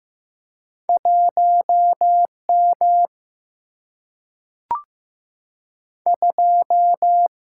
Morse code, heard as a keyed tone, about 15 wpm, 700 Hz. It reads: 1 M 2
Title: Single-Letter Single-Number 15wpm